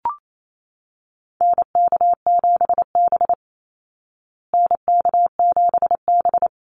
Morse code, heard as a keyed tone, about 28 wpm, 700 Hz. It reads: NK7B NK7B
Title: Call Signs Repeated 2x - 28wpm